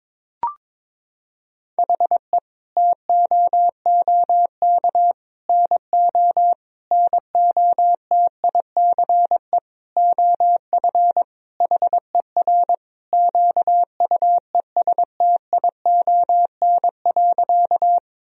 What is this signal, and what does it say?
Morse code, heard as a keyed tone, about 22 wpm, 700 Hz. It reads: HE TOOK NO NOTICE OF HER QUESTION.
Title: Sentences from Top 1000 Words - Review Part 2 - 22wpm